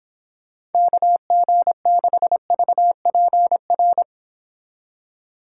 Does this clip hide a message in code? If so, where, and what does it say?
Yes: Morse code from 0.7 to 4.0 s, heard as a keyed tone, about 29 wpm, 700 Hz. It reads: KG6VPR